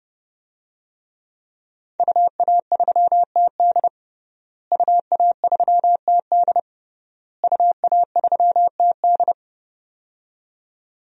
Morse code, heard as a keyed tone, about 30 wpm, 700 Hz. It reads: UA3TD UA3TD UA3TD
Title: DXCC-100 Call Signs - 5 Characters - Repeated 3x - 30wpm